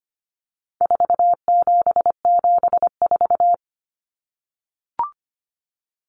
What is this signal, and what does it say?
Morse code, heard as a keyed tone, about 25 wpm, 700 Hz. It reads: 4774
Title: Common Cut Numbers - 4-Digit 25wpm